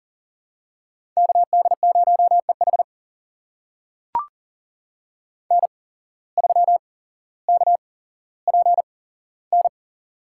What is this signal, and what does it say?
Morse code, heard as a keyed tone, about 37 wpm, spaced as for 17 wpm, 700 Hz. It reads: KD0EH N3KPN
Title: USA Call Signs - 5 Characters - 8x Character Spacing - 40wpm